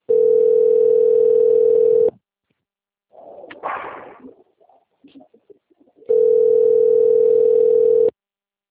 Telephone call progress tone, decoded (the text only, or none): ringback tone